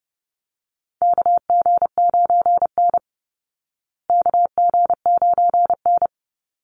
Morse code, heard as a keyed tone, about 30 wpm, 700 Hz. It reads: KG9N KG9N
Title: Call Signs Repeated 3x - 30wpm